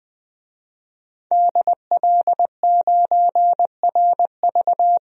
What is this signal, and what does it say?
Morse code, heard as a keyed tone, about 20 wpm, 700 Hz. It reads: DL9RV
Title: DXCC-100 Call Signs - 5 Characters - Repeated 2x - 20wpm